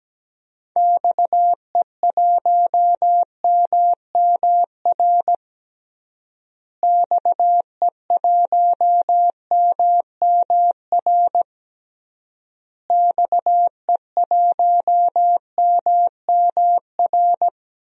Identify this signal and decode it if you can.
Morse code, heard as a keyed tone, about 17 wpm, 700 Hz. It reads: XE1MMR XE1MMR XE1MMR
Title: DXCC-100 Call Signs - 6 Characters - Repeated 3x - 17wpm